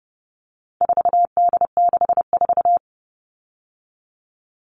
Morse code, heard as a keyed tone, about 30 wpm, 700 Hz. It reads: 4D64
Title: All Cut Numbers - 4-Digit 30wpm